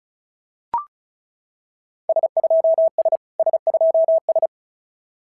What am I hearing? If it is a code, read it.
Morse code, heard as a keyed tone, about 36 wpm, 690 Hz. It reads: S2S S2S
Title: SOTA QSO Practice 35wpm